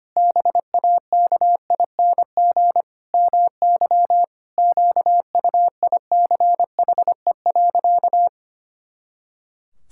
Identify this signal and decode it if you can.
Morse code, heard as a keyed tone, about 25 wpm, 700 Hz. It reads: BAKING MY QUICHE.
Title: Pangram Challenge v2 - 25wpm